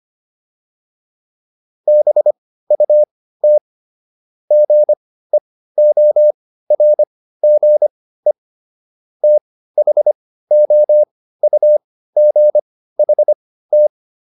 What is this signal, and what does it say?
Morse code, heard as a keyed tone, about 25 wpm, spaced as for 15 wpm, 600 Hz. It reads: BUT GEORGE THOUGHT